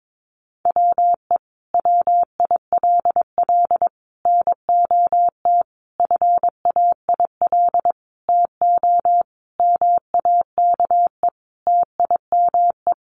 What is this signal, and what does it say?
Morse code, heard as a keyed tone, about 22 wpm, 700 Hz. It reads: WE WILL NOT FAIL TO MAKE TIME
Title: Sentences from Top 1100 Words 22wpm